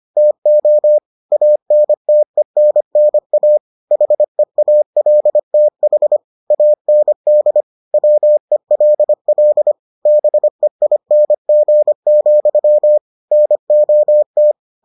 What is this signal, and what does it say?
Morse code, heard as a keyed tone, about 25 wpm, 600 Hz. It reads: TO ANTENNA HEALTH AND WELL BEING, NOT